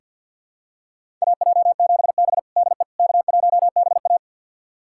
Morse code, heard as a keyed tone, about 51 wpm, 700 Hz. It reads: AJ7B DE K1BA
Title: USA Callsign DE Callsign - 4 Characters - 50wpm